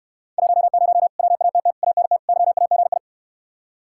Morse code, heard as a keyed tone, about 67 wpm, 700 Hz. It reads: 10 WATT ANT YAGI